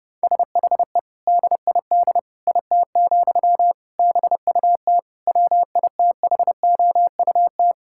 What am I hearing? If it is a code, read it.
Morse code, heard as a keyed tone, about 32 wpm, 700 Hz. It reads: SHE DID IT, BUT WITHOUT
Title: Sentences from Top 1000 Words - Review Part 6 - 30wpm